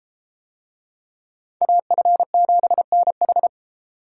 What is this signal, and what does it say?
Morse code, heard as a keyed tone, about 31 wpm, 700 Hz. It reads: AF7NH